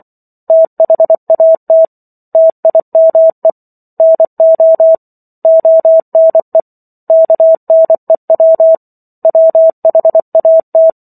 Morse code, heard as a keyed tone, about 24 wpm, 650 Hz. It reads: THAT TIME NO ONE KNEW WHAT